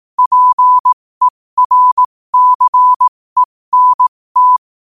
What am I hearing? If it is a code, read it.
Morse code, heard as a keyed tone, about 18 wpm, spaced as for 15 wpm, 1000 Hz. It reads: PERCENT
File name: qotdfiles / 15wpm